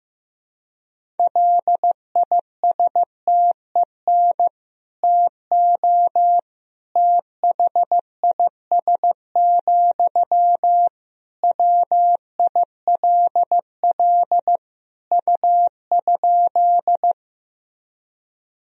Morse code, heard as a keyed tone, about 16 wpm, 700 Hz. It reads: LISTEN TO THIS, WILL U?
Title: Sentences from Top 400 Words and Abbreviations 15wpm